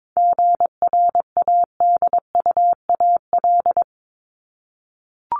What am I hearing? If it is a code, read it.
Morse code, heard as a keyed tone, about 22 wpm, 700 Hz. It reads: GRADUAL
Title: LY Suffix - Top 100 Words with Base Words 22wpm